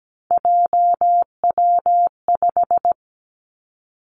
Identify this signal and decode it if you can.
Morse code, heard as a keyed tone, about 17 wpm, 700 Hz. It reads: JW5
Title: Three Challenging Letter-Number Combinations 17wpm